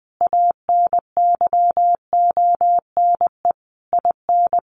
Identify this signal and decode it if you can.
Morse code, heard as a keyed tone, about 20 wpm, 700 Hz. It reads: ANYONE IN
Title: Sentences from Top 900 Words 20wpm